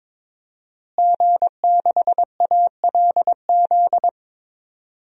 Morse code, heard as a keyed tone, about 22 wpm, 700 Hz. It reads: G6ALZ
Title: DXCC-100 Call Signs - 5 Characters - Repeated 2x - 22wpm